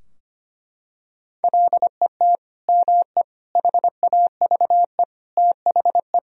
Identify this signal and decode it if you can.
Morse code, heard as a keyed tone, about 25 wpm, 700 Hz. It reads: LET ME HAVE THE
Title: Sentences from Top 1000 Words - Review Part 2 - 25wpm